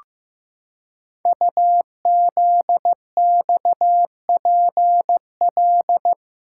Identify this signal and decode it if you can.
Morse code, heard as a keyed tone, about 15 wpm, 700 Hz. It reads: UZXPL
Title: Code Groups - 5 Letters - 15wpm